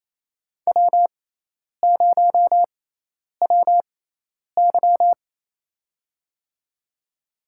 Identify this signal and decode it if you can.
Morse code, heard as a keyed tone, about 28 wpm, spaced as for 10 wpm, 700 Hz. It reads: W0WY
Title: USA Call Signs - 4 Characters - 6x Character Spacing - 28wpm